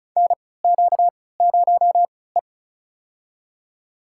Morse code, heard as a keyed tone, about 36 wpm, spaced as for 20 wpm, 700 Hz. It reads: NQ0E
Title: USA Call Signs - 4 Characters - 3x Character Spacing - 35wpm